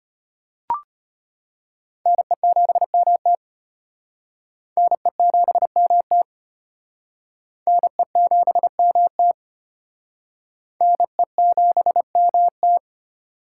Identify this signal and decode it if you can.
Morse code, heard as a keyed tone, about 33 wpm, 700 Hz. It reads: NE7MT NE7MT NE7MT NE7MT